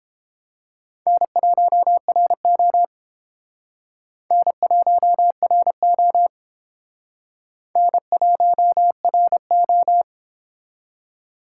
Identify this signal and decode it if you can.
Morse code, heard as a keyed tone, about 29 wpm, 700 Hz. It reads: N1RO N1RO N1RO